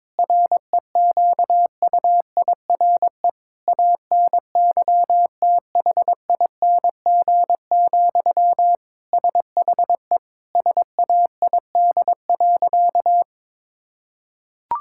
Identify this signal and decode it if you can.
Morse code, heard as a keyed tone, about 22 wpm, 700 Hz. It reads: REQUIRE ANYTHING, SHE SAID.
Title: Sentences from Top 1000 Words - Review Part 6 - 22wpm